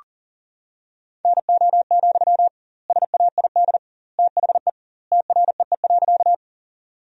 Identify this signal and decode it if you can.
Morse code, heard as a keyed tone, about 40 wpm, 700 Hz. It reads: NO, SAID THE TREE.
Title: 2x Word-Spacing - Sentences from Top 300 Words - 40wpm